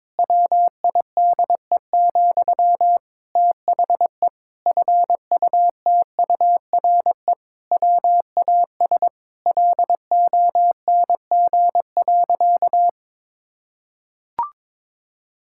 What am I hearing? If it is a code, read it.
Morse code, heard as a keyed tone, about 22 wpm, 700 Hz. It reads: WIDE, THE FUTURE WAS LONG.
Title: Sentences from Top 1000 Words - Review Part 6 - 22wpm